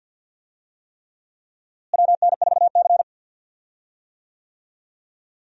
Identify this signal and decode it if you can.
Morse code, heard as a keyed tone, about 51 wpm, 700 Hz. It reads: WN4C
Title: USA Call Signs - 4 Characters - Repeated 3x - 50wpm